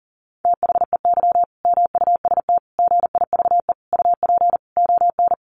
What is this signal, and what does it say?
Morse code, heard as a keyed tone, about 40 wpm, 700 Hz. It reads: THEY MUST GIVE UP ON
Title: Sentences from Top 1100 Words 40wpm